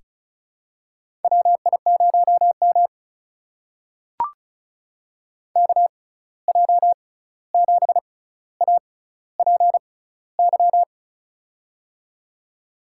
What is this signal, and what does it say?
Morse code, heard as a keyed tone, about 36 wpm, spaced as for 16 wpm, 700 Hz. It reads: WI0M KJ7APY
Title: USA Call Signs - 6x Character Spacing - 35wpm